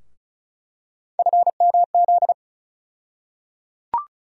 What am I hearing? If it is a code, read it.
Morse code, heard as a keyed tone, about 36 wpm, 700 Hz. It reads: FMZ
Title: USA Call Sign Postfixes - 3 Letters 35wpm